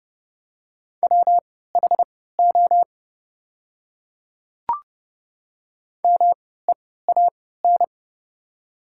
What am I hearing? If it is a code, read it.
Morse code, heard as a keyed tone, about 29 wpm, spaced as for 17 wpm, 690 Hz. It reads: WHO MEAN
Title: 3x Character Spacing - Top 200 Words - 30wpm